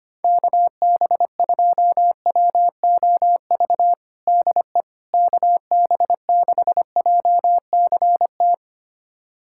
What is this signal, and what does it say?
Morse code, heard as a keyed tone, about 25 wpm, 700 Hz. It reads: KB2WOV DE KB6JCT
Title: USA Callsign DE Callsign - 6 Characters - 25wpm